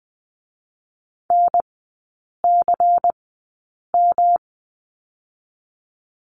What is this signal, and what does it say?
Morse code, heard as a keyed tone, about 21 wpm, spaced as for 8 wpm, 700 Hz. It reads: NCM
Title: 107 Lesson 18e - Letter m - Three Characters 20wpm